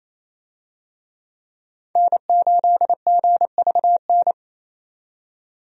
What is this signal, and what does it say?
Morse code, heard as a keyed tone, about 28 wpm, 700 Hz. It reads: N8GVN
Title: USA Call Signs - 5 Characters - 28wpm